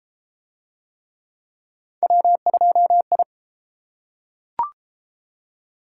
Morse code, heard as a keyed tone, about 29 wpm, 700 Hz. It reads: W2I